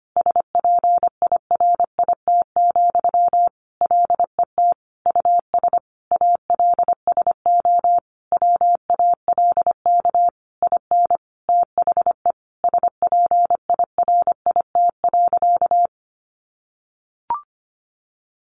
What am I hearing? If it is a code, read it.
Morse code, heard as a keyed tone, about 25 wpm, 700 Hz. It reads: SPIRIT, LET US ALSO WALK IN THE SPIRIT.
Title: Sentences from Top 1400 Words 25wpm